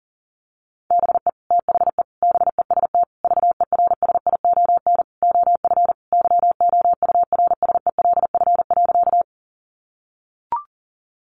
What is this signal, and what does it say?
Morse code, heard as a keyed tone, about 40 wpm, 700 Hz. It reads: BE THE BEST VERSION OF YOURSELF.
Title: Sentences from Top 1400 Words 40wpm